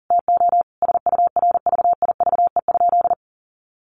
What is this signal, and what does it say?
Morse code, heard as a keyed tone, about 40 wpm, 700 Hz. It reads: TO SURVIVE?
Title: A Sound Of Thunder 40wpm